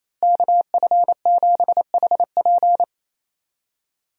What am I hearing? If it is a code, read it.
Morse code, heard as a keyed tone, about 28 wpm, 700 Hz. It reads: KF7HP
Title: USA Call Signs - 5 Characters - Repeated 3x - 28wpm